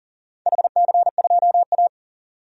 Morse code, heard as a keyed tone, about 37 wpm, 700 Hz. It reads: HC2A
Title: DXCC-100 Call Signs - 4 Characters - 40wpm